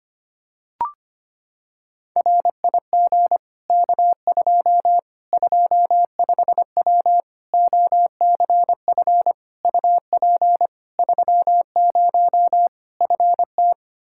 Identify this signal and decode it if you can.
Morse code, heard as a keyed tone, about 25 wpm, 700 Hz. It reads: RIG K2 25W OCF UP 30 FT
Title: Practice QSO - Exchange 2 - Info - 25wpm